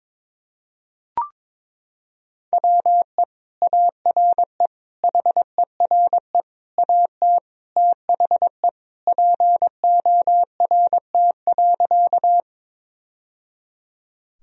Morse code, heard as a keyed tone, about 22 wpm, 700 Hz. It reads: WE ARE HERE AT THE PORT.
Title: Sentences from Top 1000 Words - Review Part 1 - 22wpm